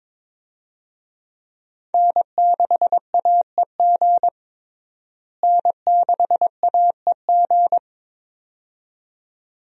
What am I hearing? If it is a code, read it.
Morse code, heard as a keyed tone, about 21 wpm, 700 Hz. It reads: N6AEG N6AEG
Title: USA Call Signs - 5 Characters - Repeated 2x - 22wpm